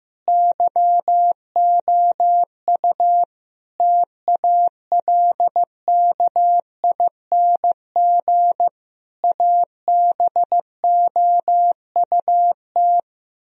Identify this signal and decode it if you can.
Morse code, heard as a keyed tone, about 15 wpm, 700 Hz. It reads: YOU TALKING ABOUT
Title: Speed-Racing - Sets of 5 Words 15wpm